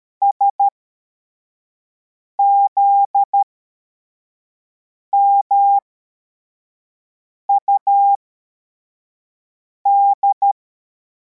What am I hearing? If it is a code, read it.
Morse code, heard as a keyed tone, about 13 wpm, spaced as for 4 wpm, 800 Hz. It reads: SZMUD